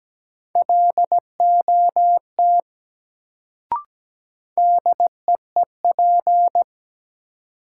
Morse code, heard as a keyed tone, about 17 wpm, 700 Hz. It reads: LOT DEEP
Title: Sentences from Top 500 Words - New Word Drill - 17wpm